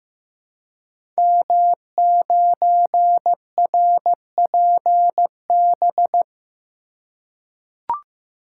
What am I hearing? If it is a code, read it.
Morse code, heard as a keyed tone, about 15 wpm, 700 Hz. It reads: M9RPB
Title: Code Groups - 5 Letter:Numbers - 15wpm